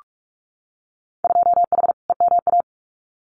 Morse code, heard as a keyed tone, about 46 wpm, 700 Hz. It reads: 2H EMA
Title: Winter Field Day Exchanges 45wpm